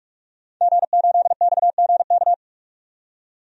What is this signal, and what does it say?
Morse code, heard as a keyed tone, about 42 wpm, 700 Hz. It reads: G8XGK
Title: DXCC-100 Call Signs - 5 Characters - 45wpm